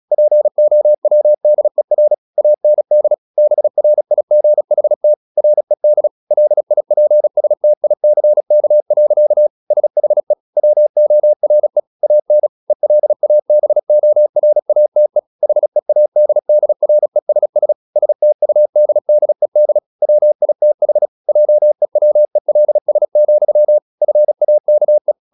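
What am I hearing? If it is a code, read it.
Morse code, heard as a keyed tone, about 36 wpm, 600 Hz. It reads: POWDER AND BRIGHT RED LIPSTICK. SHE WORE AN ELABORATE HEADDRESS STUDDED WITH JEWELS, FAKE
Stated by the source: Morse practice at 36 wpm, text of Ken Follett's novel Never, from CW med Gnister.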